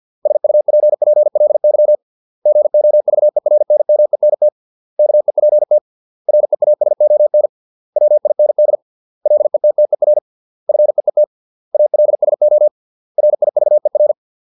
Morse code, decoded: SUPPLY GOVERNMENT KEPT REASON WIND LETTER FEET ALSO RIVER

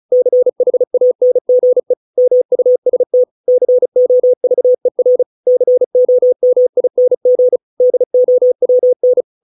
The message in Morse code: CHANGE MUST COVER COMING DOWN

